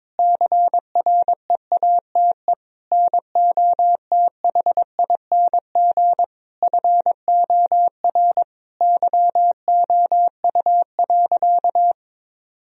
Morse code, heard as a keyed tone, about 22 wpm, 700 Hz. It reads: CREATE NOTHING FOR YOU.
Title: Sentences from Top 1000 Words - Review Part 3 - 22wpm